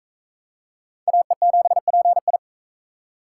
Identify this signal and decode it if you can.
Morse code, heard as a keyed tone, about 37 wpm, 700 Hz. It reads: AE7PI